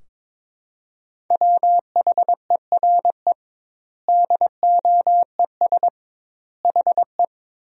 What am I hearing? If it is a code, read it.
Morse code, heard as a keyed tone, about 22 wpm, 700 Hz. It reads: WHERE DOES HE
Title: 2x Word-Spacing - Sentences from Top 200 Words - 22wpm